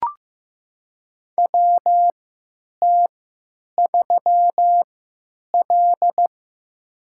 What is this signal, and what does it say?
Morse code, heard as a keyed tone, about 15 wpm, spaced as for 8 wpm, 700 Hz. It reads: WT3L
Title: USA Call Signs - 4 Characters - 3x Character Spacing - 15wpm